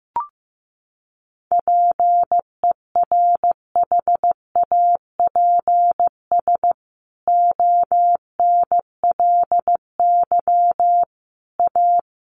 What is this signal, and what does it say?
Morse code, heard as a keyed tone, about 15 wpm, 700 Hz. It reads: PERHAPS ONLY A
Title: A Sound Of Thunder 15wpm